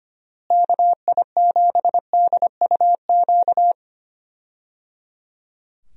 Morse code, heard as a keyed tone, about 25 wpm, 700 Hz. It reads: KI7DUQ
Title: USA Call Signs - 6 Characters - Repeated 3x - 25wpm